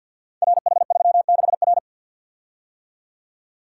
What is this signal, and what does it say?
Morse code, heard as a keyed tone, about 51 wpm, 700 Hz. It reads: RH36R
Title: Code Groups - 5 Letter:Numbers - 50wpm